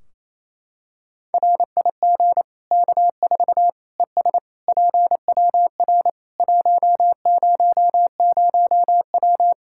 Morse code, heard as a keyed tone, about 28 wpm, 700 Hz. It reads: RIG K4 ES PWR 100W
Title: K1USN SST QSO Practice 28wpm